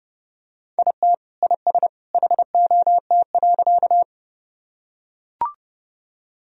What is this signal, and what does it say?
Morse code, heard as a keyed tone, about 30 wpm, 700 Hz. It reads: IT IS HOT.
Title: Sentences from Top 1000 Words - Review Part 3 - 30wpm